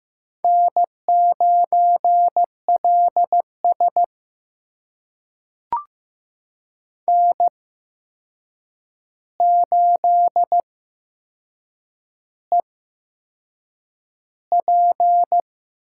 Morse code, heard as a keyed tone, about 15 wpm, spaced as for 6 wpm, 700 Hz. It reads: N9LS N8EP
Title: USA Call Signs - 8x Character Spacing - 15wpm